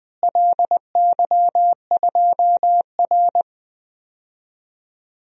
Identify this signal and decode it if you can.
Morse code, heard as a keyed tone, about 20 wpm, 700 Hz. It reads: LY2R
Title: DXCC-100 Call Signs - 4 Characters - Repeated 2x - 20wpm